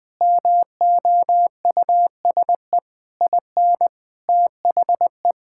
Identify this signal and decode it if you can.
Morse code, heard as a keyed tone, about 20 wpm, 700 Hz. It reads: MOUSE IN THE
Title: Sentences from Top 1000 Words - Review Part 5 - 20wpm